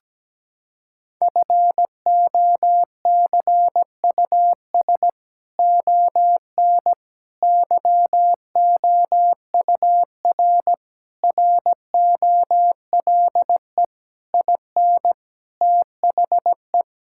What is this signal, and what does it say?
Morse code, heard as a keyed tone, about 17 wpm, 700 Hz. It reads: FOCUS ON YOUR ROLE IN THE
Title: Sentences from Top 1000 Words - Review Part 6 - 17wpm